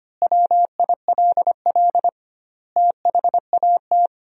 Morse code, heard as a keyed tone, about 25 wpm, 700 Hz. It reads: WILL THAT
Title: 2x Word-Spacing - Sentences from Top 100 Words - 25wpm